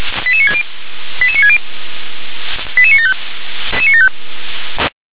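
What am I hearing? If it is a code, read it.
Five-tone selective call (ZVEI), heard as quick runs of five tones, tones of about 70 ms. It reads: 88960 88960 88975
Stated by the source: Five-tone ZVEI selective call, received on SDR.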